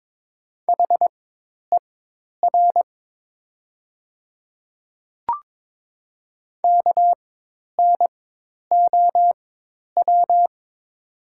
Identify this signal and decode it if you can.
Morse code, heard as a keyed tone, about 22 wpm, spaced as for 10 wpm, 700 Hz. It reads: HER KNOW
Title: 4x Character Spacing - Top 300 Words - 22wpm